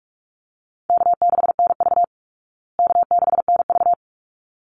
Morse code, heard as a keyed tone, about 44 wpm, 700 Hz. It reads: K6NV K6NV
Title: Easy Call Signs Repeated 2x - 45wpm